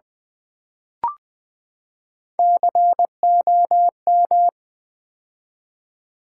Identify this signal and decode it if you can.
Morse code, heard as a keyed tone, about 20 wpm, 700 Hz. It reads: COM
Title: Build A Word - 3 Letters At A Time - 9-Letter Words - 20wpm